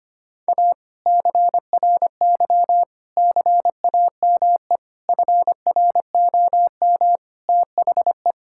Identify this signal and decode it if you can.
Morse code, heard as a keyed tone, about 25 wpm, 700 Hz. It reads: A CRY CAME FROM THE
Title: Sentences from Top 1000 Words - Review Part 5 - 25wpm